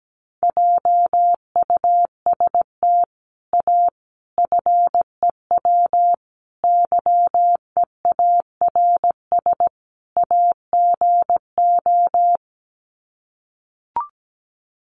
Morse code, heard as a keyed tone, about 17 wpm, 700 Hz. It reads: JUST A FEW YEARS AGO
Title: Sets of 5 Words Encore 17wpm